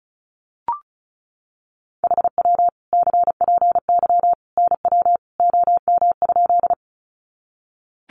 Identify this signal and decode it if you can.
Morse code, heard as a keyed tone, about 34 wpm, 700 Hz. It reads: HW CPY NW OM?
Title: Common QSO Phrases 35wpm